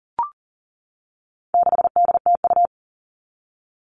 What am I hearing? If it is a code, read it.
Morse code, heard as a keyed tone, about 40 wpm, 700 Hz. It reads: 6DTU
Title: All Cut Numbers - 4-Digit 40wpm